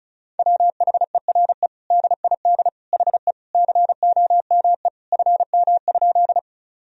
Morse code, heard as a keyed tone, about 36 wpm, 700 Hz. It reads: WHERE DID HE COME FM?
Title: Sentences from Top 200 Words and Abbreviations 35wpm